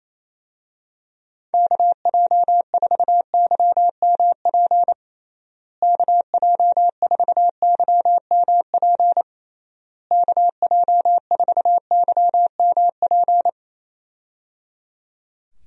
Morse code, heard as a keyed tone, about 28 wpm, 700 Hz. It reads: KJ4YMP KJ4YMP KJ4YMP
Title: USA Call Signs - 6 Characters - Repeated 3x - 28wpm